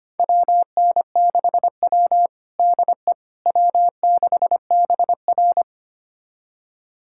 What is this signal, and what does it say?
Morse code, heard as a keyed tone, about 25 wpm, 700 Hz. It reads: WN6W DE W6BR
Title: USA Callsign DE Callsign - 4 Characters - 25wpm